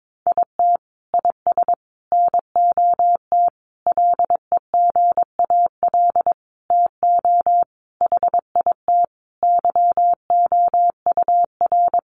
Morse code, heard as a keyed tone, about 22 wpm, 700 Hz. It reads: IT IS NOT LEGAL TO HIT YOUR
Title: Sentences from Top 1200 Words 22wpm